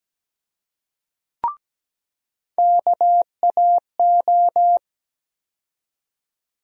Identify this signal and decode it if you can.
Morse code, heard as a keyed tone, about 17 wpm, 700 Hz. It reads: KAO
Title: USA Call Sign Postfixes - 3 Letters 17wpm